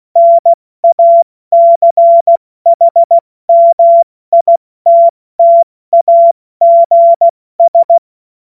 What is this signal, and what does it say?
Morse code, heard as a keyed tone, about 16 wpm, 680 Hz. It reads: NACHMITTAGS
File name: Witz-08-Juli-2025-70-F80